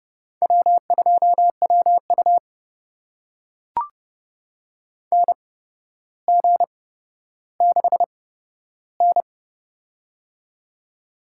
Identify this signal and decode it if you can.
Morse code, heard as a keyed tone, about 30 wpm, spaced as for 13 wpm, 700 Hz. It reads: W2WU NG6N
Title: USA Call Signs - 8x Character Spacing - 30wpm